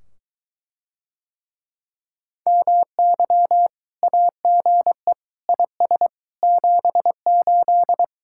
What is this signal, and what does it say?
Morse code, heard as a keyed tone, about 22 wpm, 700 Hz. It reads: MY AGE IS 78